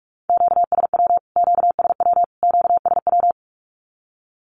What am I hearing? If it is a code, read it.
Morse code, heard as a keyed tone, about 44 wpm, 700 Hz. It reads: QSW QSW QSW
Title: Q-Codes Repeated 3x 45wpm